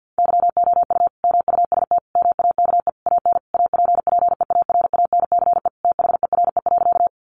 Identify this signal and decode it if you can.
Morse code, heard as a keyed tone, about 49 wpm, 700 Hz. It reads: YOU MUST MAKE AN APPEARANCE THERE.
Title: Sentences from Top 1100 Words 50wpm